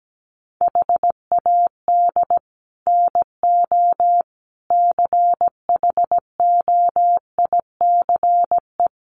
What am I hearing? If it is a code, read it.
Morse code, heard as a keyed tone, about 17 wpm, 700 Hz. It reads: HAD NO CHOICE
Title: Sets of 5 Words 17wpm